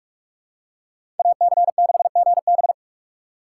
Morse code, heard as a keyed tone, about 46 wpm, 700 Hz. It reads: AC6GB
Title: USA Call Signs - 5 Characters - 2x Character Spacing - 45wpm